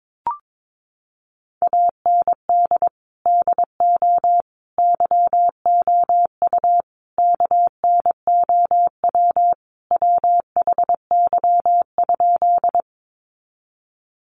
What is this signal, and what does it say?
Morse code, heard as a keyed tone, about 22 wpm, 700 Hz. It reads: AND DO YOU KNOW WHY?
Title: Sentences from Top 200 Words 22wpm